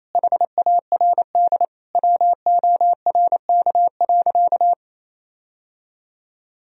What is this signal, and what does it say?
Morse code, heard as a keyed tone, about 28 wpm, 700 Hz. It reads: HARD WORK.
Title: Sentences from Top 1000 Words - Review Part 3 - 28wpm